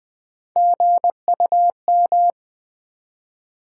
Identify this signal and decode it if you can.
Morse code, heard as a keyed tone, about 20 wpm, 700 Hz. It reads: GUM
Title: Easy Words with Hints - 20wpm